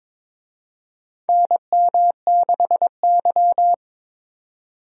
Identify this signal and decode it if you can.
Morse code, heard as a keyed tone, about 23 wpm, 700 Hz. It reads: NM6Y
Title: USA Call Signs - 4 Characters - 22wpm